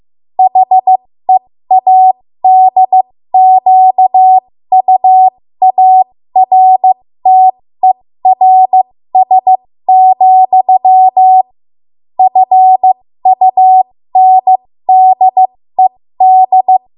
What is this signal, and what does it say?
Morse code, heard as a keyed tone, about 15 wpm, spaced as for 13 wpm, 750 Hz. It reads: HEADQUARTERS, FUNDED
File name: Morse / 13WPM